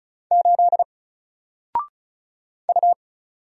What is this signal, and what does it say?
Morse code, heard as a keyed tone, about 36 wpm, 700 Hz. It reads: 8 U
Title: Single Character (Letter:Number:Punctuation:Prosign) 35wpm